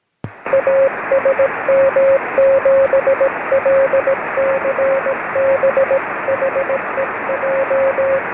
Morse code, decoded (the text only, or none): A SM7LCB HEJ